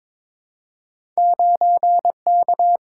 Morse code, heard as a keyed tone, about 22 wpm, 700 Hz. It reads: 9K
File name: practice / 22wpm / Two Letter-Number Combinations - 22wpm